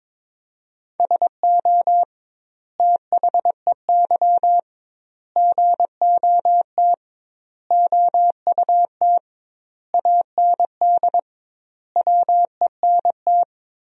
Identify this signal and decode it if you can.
Morse code, heard as a keyed tone, about 22 wpm, 700 Hz. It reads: SO THEY GOT OUT AND WENT
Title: 2x Word-Spacing - Sentences from Top 300 Words - 22wpm